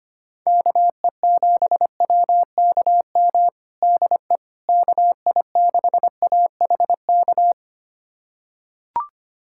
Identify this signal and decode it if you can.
Morse code, heard as a keyed tone, about 25 wpm, 700 Hz. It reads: KE7WKM DE KI6AHK
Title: USA Callsign DE Callsign - 6 Characters - 25wpm